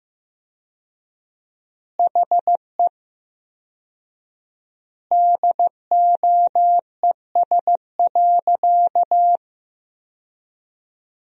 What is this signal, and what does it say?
Morse code, heard as a keyed tone, about 16 wpm, 700 Hz. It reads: HE DOES.
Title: 4x Word-Spacing - Sentences from Top 500 Words - 15wpm